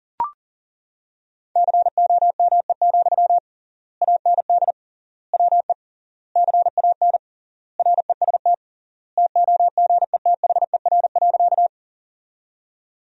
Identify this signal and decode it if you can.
Morse code, heard as a keyed tone, about 40 wpm, 700 Hz. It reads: COME, AND WE CAN REST TOGETHER.
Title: 3x Word-Spacing - Sentences from Top 500 Words - 40wpm